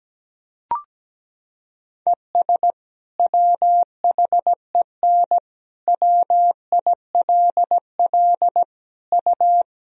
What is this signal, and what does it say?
Morse code, heard as a keyed tone, about 17 wpm, 700 Hz. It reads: ES WHEN WILL U
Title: Sentences from Top 100 Words and Abbreviations 17wpm